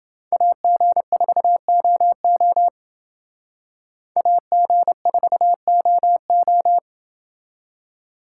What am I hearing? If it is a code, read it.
Morse code, heard as a keyed tone, about 28 wpm, 700 Hz. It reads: AG4OO AG4OO